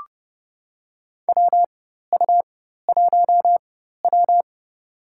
Morse code, heard as a keyed tone, about 31 wpm, spaced as for 14 wpm, 700 Hz. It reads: WU1W
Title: USA Call Signs - 4 Characters - 4x Character Spacing - 30wpm